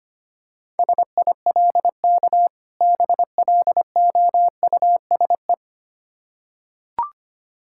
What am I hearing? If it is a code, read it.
Morse code, heard as a keyed tone, about 25 wpm, 700 Hz. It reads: SILK BLOUSE
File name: practice / 25wpm / Difficult Words with Hints - 25wpm